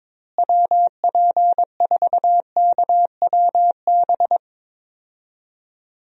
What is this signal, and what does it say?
Morse code, heard as a keyed tone, about 21 wpm, 700 Hz. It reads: WP4KWB
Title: USA Call Signs - 6 Characters - Repeated 3x - 22wpm